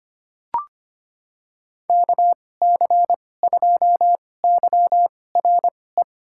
Morse code, heard as a keyed tone, about 25 wpm, spaced as for 18 wpm, 700 Hz. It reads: KC2YRE
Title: USA Call Signs - 6 Characters - 2x Character Spacing - 25wpm